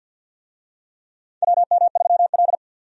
Morse code, heard as a keyed tone, about 49 wpm, 700 Hz. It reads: WM3L
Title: USA Call Signs - 4 Characters - 50wpm